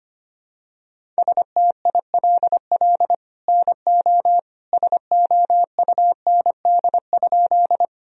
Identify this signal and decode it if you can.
Morse code, heard as a keyed tone, about 25 wpm, 700 Hz. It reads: STILL NO SOUND?
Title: Sentences from Top 1000 Words - Review Part 1 - 25wpm